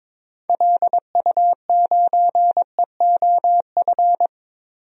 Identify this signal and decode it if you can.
Morse code, heard as a keyed tone, about 22 wpm, 700 Hz. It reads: LU9EOF
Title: DXCC-100 Call Signs - 6 Characters - 22wpm